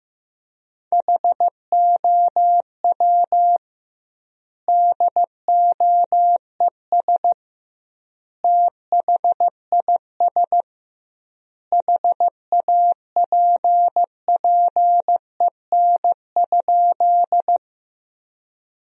Morse code, heard as a keyed tone, about 15 wpm, 700 Hz. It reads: HOW DOES THIS HAPPEN?
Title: 2x Word-Spacing - Sentences from Top 400 Words - 15wpm